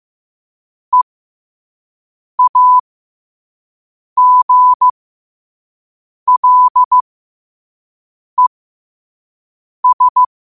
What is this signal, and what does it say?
Morse code, heard as a keyed tone, about 15 wpm, spaced as for 5 wpm, 1000 Hz. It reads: EAGLES
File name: qotdfiles / 5wpm